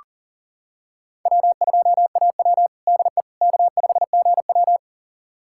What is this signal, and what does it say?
Morse code, heard as a keyed tone, about 40 wpm, 700 Hz. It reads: W2AW DE K5GW
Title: USA Callsign DE Callsign - 4 Characters - 40wpm